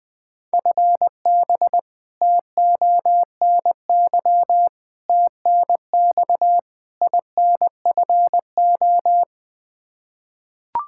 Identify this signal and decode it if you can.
Morse code, heard as a keyed tone, about 20 wpm, 700 Hz. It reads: FB TONY TNX INFO
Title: Practice QSO - Exchange 3 - More Info - 20wpm